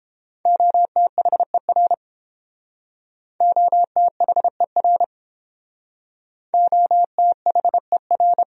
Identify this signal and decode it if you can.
Morse code, heard as a keyed tone, about 31 wpm, 700 Hz. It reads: OTHER OTHER OTHER